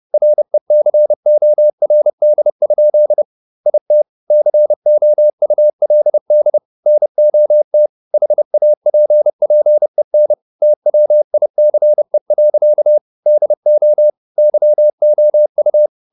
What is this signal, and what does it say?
Morse code, heard as a keyed tone, about 30 wpm, 600 Hz. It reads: RECORD? IT COULD NOT HAPPEN TWICE. DO YOU
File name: the old man and the sea / the old man and the sea 1 30wpm